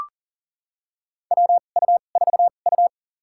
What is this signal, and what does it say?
Morse code, heard as a keyed tone, about 37 wpm, spaced as for 28 wpm, 700 Hz. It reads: WU4U
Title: USA Call Signs - 4 Characters - 2x Character Spacing - 40wpm